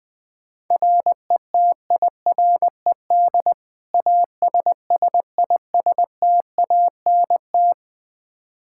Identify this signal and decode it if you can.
Morse code, heard as a keyed tone, about 20 wpm, 700 Hz. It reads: RETIRED ASSISTANT
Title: Practice QSO - Exchange 3 - More Info - 20wpm